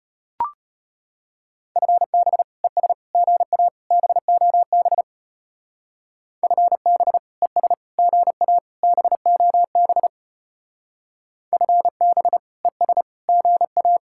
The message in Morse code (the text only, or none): FB ES GA BOB FB ES GA BOB FB ES GA